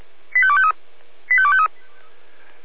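Five-tone selective call (ZVEI), heard as quick runs of five tones, tones of about 70 ms. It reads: 85253 85253
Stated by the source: ZVEI five-tone selective call received on SDR, from the feuerwehr-orlamuende fire brigade audio server.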